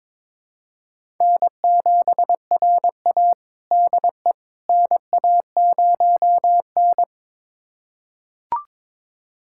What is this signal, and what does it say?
Morse code, heard as a keyed tone, about 22 wpm, 700 Hz. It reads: N7RA DE NA0N
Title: USA Callsign DE Callsign - 4 Characters - 22wpm